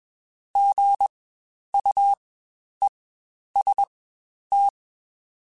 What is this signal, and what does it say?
Morse code, heard as a keyed tone, about 21 wpm, spaced as for 10 wpm, 800 Hz. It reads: GUEST
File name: Spaced Alphas